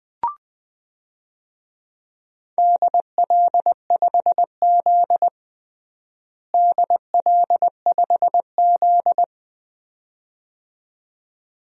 Morse code, heard as a keyed tone, about 20 wpm, 700 Hz. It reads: DL5Z DL5Z
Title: DXCC-100 Call Signs - 4 Characters - Repeated 2x - 20wpm